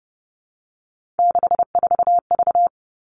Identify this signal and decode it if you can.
Morse code, heard as a keyed tone, about 30 wpm, 700 Hz. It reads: 64V
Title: All Cut Numbers - 3-Digit 30wpm